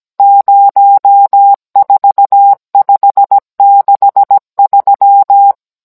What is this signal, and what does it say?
Morse code, heard as a keyed tone, about 17 wpm, 800 Hz. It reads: 04563
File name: Cours-NR18-16mm